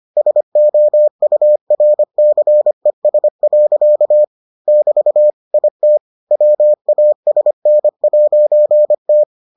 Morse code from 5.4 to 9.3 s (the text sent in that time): IT WASN'T